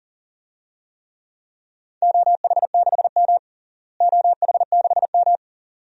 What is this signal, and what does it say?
Morse code, heard as a keyed tone, about 40 wpm, 700 Hz. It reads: OH6M OH6M
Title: DXCC-100 Call Signs - 4 Characters - Repeated 2x - 40wpm